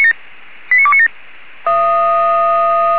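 Five-tone selective call (ZVEI), heard as quick runs of five tones, tones of about 70 ms. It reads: 89297
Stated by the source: ZVEI five-tone selective call received with SDR.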